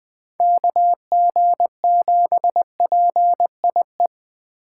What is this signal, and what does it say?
Morse code, heard as a keyed tone, about 20 wpm, 700 Hz. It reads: KG7PIE